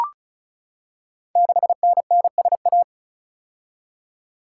Morse code, heard as a keyed tone, about 36 wpm, 690 Hz. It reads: 6NNSA
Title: Code Groups - 5 Letter:Numbers - 35wpm